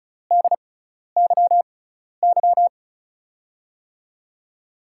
Morse code, heard as a keyed tone, about 36 wpm, spaced as for 12 wpm, 700 Hz. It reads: DYY
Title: USA Call Signs - 5 Characters - 6x Character Spacing - 35wpm